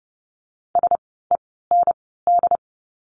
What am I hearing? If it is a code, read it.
Morse code, heard as a keyed tone, about 30 wpm, 700 Hz. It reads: S E N D
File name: Lesson 16f - Letter c - Words 30wpm